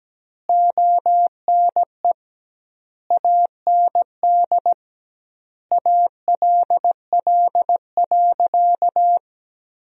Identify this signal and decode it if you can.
Morse code, heard as a keyed tone, about 17 wpm, 700 Hz. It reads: ONE AND ALL.
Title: 2x Word-Spacing - Sentences from Top 200 Words - 17wpm